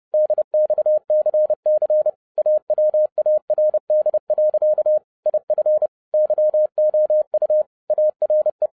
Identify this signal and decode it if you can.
Morse code, heard as a keyed tone, about 30 wpm, 610 Hz. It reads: DXCC AWARD. IF YOU ARE
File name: 2012-01-mp3 / n12-I DIDNT MISS THE SOLAR CYCLE 24-30wpm